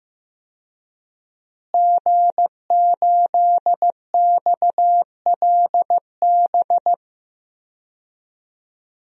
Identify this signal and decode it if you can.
Morse code, heard as a keyed tone, about 15 wpm, 700 Hz. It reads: G8XLB
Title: DXCC-100 Call Signs - 5 Characters - Repeated 3x - 15wpm